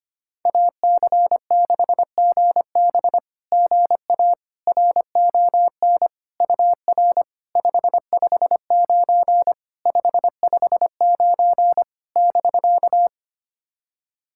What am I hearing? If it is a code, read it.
Morse code, heard as a keyed tone, about 25 wpm, 700 Hz. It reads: AC6GB GA RON UR 559 559 <BK>
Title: SOTA QSO Practice 25wpm